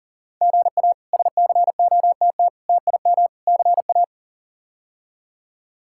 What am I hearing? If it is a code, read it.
Morse code, heard as a keyed tone, about 40 wpm, 700 Hz. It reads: GA SCOTT TIM CA
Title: K1USN SST QSO Practice - 40wpm